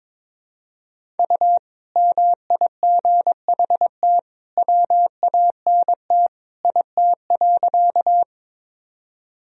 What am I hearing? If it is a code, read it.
Morse code, heard as a keyed tone, about 21 wpm, 700 Hz. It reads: U MIGHT WANT IT.
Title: Sentences from Top 300 Words and Abbreviations 22wpm